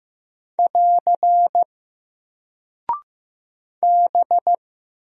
Morse code, heard as a keyed tone, about 15 wpm, 700 Hz. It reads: <AR> B
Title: Single Character (Letter:Number:Punctuation:Prosign) Alt 15wpm